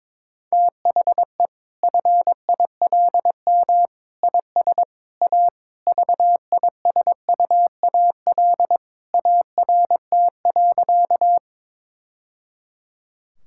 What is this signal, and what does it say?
Morse code, heard as a keyed tone, about 22 wpm, 700 Hz. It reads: THE FILM IS A VISUAL ART.
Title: Sentences from Top 1000 Words - Review Part 6 - 22wpm